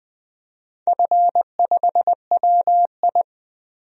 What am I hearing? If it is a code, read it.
Morse code, heard as a keyed tone, about 20 wpm, 700 Hz. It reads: F5WI
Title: DXCC-100 Call Signs - 4 Characters - Repeated 3x - 20wpm